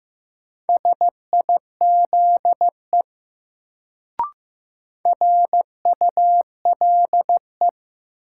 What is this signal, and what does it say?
Morse code, heard as a keyed tone, about 15 wpm, 700 Hz. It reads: SIZE RULE
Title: Sentences from Top 400 Words - New Word Drill - 15wpm